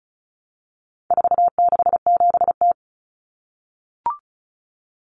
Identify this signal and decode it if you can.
Morse code, heard as a keyed tone, about 34 wpm, 700 Hz. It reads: 467T
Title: Common Cut Numbers - 4-Digit 35wpm